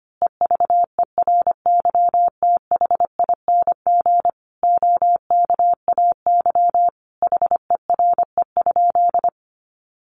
Morse code, heard as a keyed tone, about 25 wpm, 700 Hz. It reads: EVERYTHING OKAY HERE?
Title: A Sound Of Thunder 25wpm